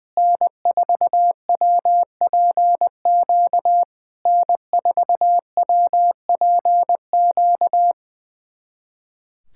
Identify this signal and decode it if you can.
Morse code, heard as a keyed tone, about 20 wpm, 700 Hz. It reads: N4WPQ N4WPQ
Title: SOTA QSO Practice 20wpm